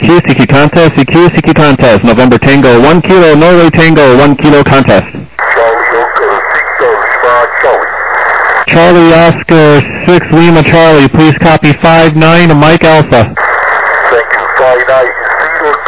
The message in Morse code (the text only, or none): none